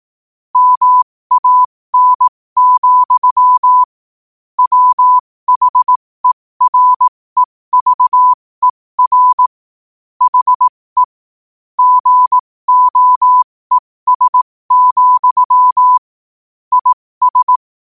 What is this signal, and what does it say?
Morse code, heard as a keyed tone, about 18 wpm, spaced as for 15 wpm, 1000 Hz. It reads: MAN, WHEREVER HE GOES, IS
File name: qotdfiles / 15wpm